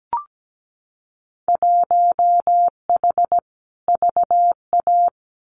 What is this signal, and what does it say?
Morse code, heard as a keyed tone, about 17 wpm, 700 Hz. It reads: 1H VA
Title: Winter Field Day Exchanges 17wpm